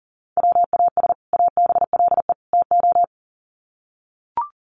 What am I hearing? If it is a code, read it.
Morse code, heard as a keyed tone, about 37 wpm, 700 Hz. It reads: WAS ABLE TO
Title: Sets of 3 Words 40wpm